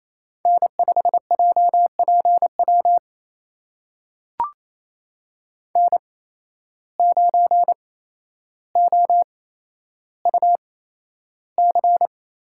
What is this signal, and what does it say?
Morse code, heard as a keyed tone, about 28 wpm, spaced as for 12 wpm, 700 Hz. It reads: N5JPW N9OUC
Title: USA Call Signs - 5 Characters - 8x Character Spacing - 28wpm